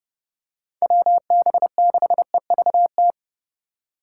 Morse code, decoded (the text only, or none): WB6EVT